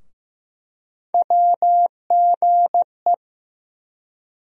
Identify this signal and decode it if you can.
Morse code, heard as a keyed tone, about 15 wpm, 700 Hz. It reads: WGE
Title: USA Call Sign Postfixes - 3 Letters 15wpm